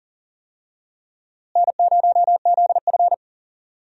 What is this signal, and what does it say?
Morse code, heard as a keyed tone, about 40 wpm, 700 Hz. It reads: N0ZF
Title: USA Call Signs - 4 Characters - Repeated 3x - 40wpm